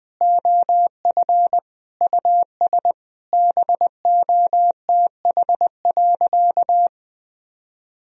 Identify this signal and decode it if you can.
Morse code, heard as a keyed tone, about 20 wpm, 700 Hz. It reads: OF US BOTH.
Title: Sentences from Top 1000 Words - Review Part 4 - 20wpm